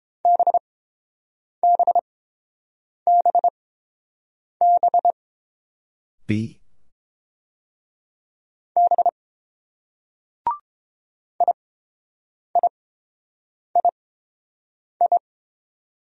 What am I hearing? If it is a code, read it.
Morse code, heard as a keyed tone, about 30 wpm, spaced as for 7 wpm, 700 Hz. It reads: BBBB B IIII